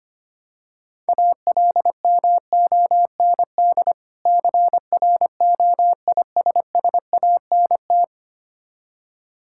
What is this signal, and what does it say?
Morse code, heard as a keyed tone, about 25 wpm, 700 Hz. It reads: ALMOND CROISSANT
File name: practice / 25wpm / Difficult Words with Hints - 25wpm